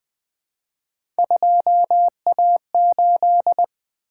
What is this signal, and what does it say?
Morse code, heard as a keyed tone, about 21 wpm, 700 Hz. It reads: 2A8
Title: Three Letter-Number Combinations - 20wpm